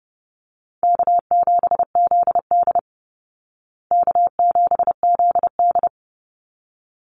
Morse code, heard as a keyed tone, about 30 wpm, 700 Hz. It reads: K7ZD K7ZD
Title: Easy Call Signs Repeated 2x - 30wpm